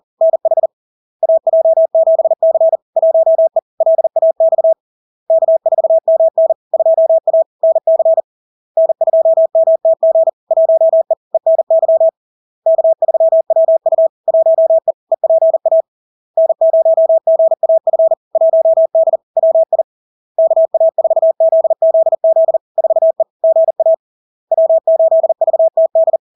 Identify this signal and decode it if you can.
Morse code, heard as a keyed tone, about 40 wpm, 650 Hz. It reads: NH AJ7C 1E LAX K4MN 2A NC N2MTG 1E ENY K3WU 1E EPA N0GAF 1D WI KA4ZZZ 4E GA W8VTD
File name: LARK-FD-2022-Log-in-CW-40wpm-650-Hz